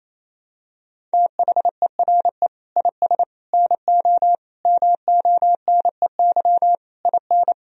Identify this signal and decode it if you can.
Morse code, heard as a keyed tone, about 28 wpm, 700 Hz. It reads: THERE IS NO MONEY IN
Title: Sentences from Top 1000 Words - Review Part 3 - 28wpm